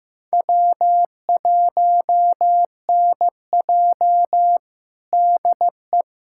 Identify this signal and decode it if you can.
Morse code, heard as a keyed tone, about 16 wpm, 700 Hz. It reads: W1NJ DE
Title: USA Callsign DE Callsign - 4 Characters - 15wpm